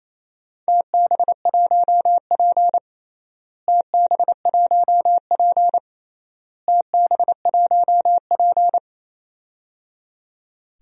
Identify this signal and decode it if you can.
Morse code, heard as a keyed tone, about 28 wpm, 700 Hz. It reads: TB1P TB1P TB1P
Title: DXCC-100 Call Signs - 4 Characters - Repeated 3x - 28wpm